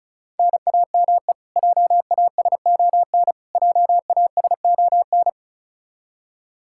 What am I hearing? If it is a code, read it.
Morse code, heard as a keyed tone, about 34 wpm, 700 Hz. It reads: NAME JASON JASON
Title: K1USN SST QSO Practice 35wpm